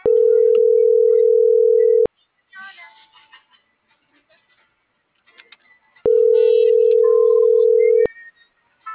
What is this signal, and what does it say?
Ringback tone (North America), 440 + 480 Hz.